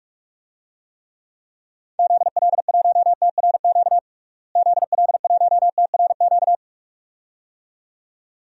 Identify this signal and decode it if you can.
Morse code, heard as a keyed tone, about 46 wpm, 700 Hz. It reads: ZL1TRQ ZL1TRQ
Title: DXCC-100 Call Signs - 6 Characters - Repeated 2x - 45wpm